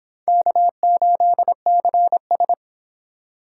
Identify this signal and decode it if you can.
Morse code, heard as a keyed tone, about 25 wpm, 700 Hz. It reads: K8CS